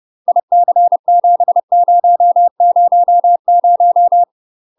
Morse code, heard as a keyed tone, about 30 wpm, 700 Hz. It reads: IC7000